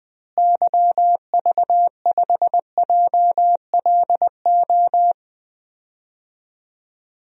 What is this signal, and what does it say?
Morse code, heard as a keyed tone, about 20 wpm, 700 Hz. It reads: YV5JLO
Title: DXCC-100 Call Signs - 6 Characters - Repeated 3x - 20wpm